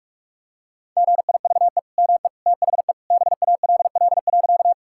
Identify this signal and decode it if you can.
Morse code, heard as a keyed tone, about 45 wpm, 700 Hz. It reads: GIVE ME THE BALL.
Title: Sentences from Top 1000 Words - Review Part 6 - 45wpm